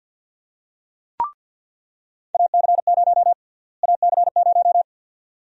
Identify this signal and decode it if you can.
Morse code, heard as a keyed tone, about 48 wpm, 700 Hz. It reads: AC0 AC0
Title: USA Call Sign Prefixes - Repeated 2x - 50wpm